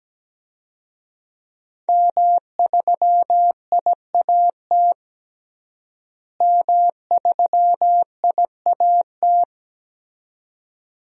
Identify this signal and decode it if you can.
Morse code, heard as a keyed tone, about 17 wpm, 700 Hz. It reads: M3IAT M3IAT
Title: DXCC-100 Call Signs - 5 Characters - Repeated 2x - 17wpm